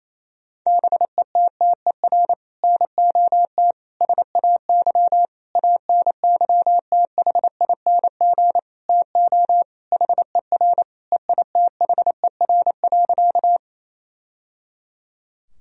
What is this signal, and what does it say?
Morse code, heard as a keyed tone, about 27 wpm, 700 Hz. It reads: BETTER NOT SAY ANYTHING TO HER EITHER.
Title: Sentences from Top 1000 Words - Review Part 4 - 28wpm